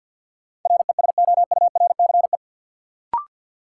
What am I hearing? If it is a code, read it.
Morse code, heard as a keyed tone, about 52 wpm, 700 Hz. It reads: RESOURCE
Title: Sentences from Top 1300 Words - New Word Drill Alt - 50wpm